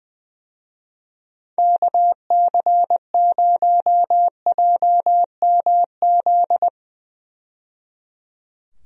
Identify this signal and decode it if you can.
Morse code, heard as a keyed tone, about 21 wpm, 700 Hz. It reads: KC0JMZ
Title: USA Call Signs - 6 Characters - 20wpm